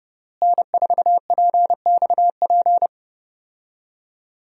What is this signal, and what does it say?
Morse code, heard as a keyed tone, about 30 wpm, 700 Hz. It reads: N4PXP
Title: USA Call Signs - 5 Characters - Repeated 3x - 30wpm